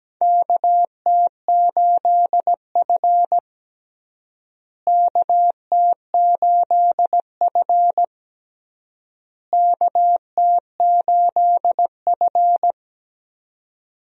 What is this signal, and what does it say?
Morse code, heard as a keyed tone, about 17 wpm, 700 Hz. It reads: KT8F KT8F KT8F
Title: USA Call Signs - 4 Characters - Repeated 3x - 17wpm